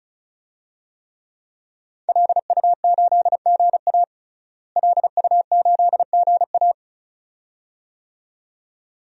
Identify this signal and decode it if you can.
Morse code, heard as a keyed tone, about 36 wpm, 700 Hz. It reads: LU8GA LU8GA
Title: DXCC-100 Call Signs - 5 Characters - Repeated 2x - 35wpm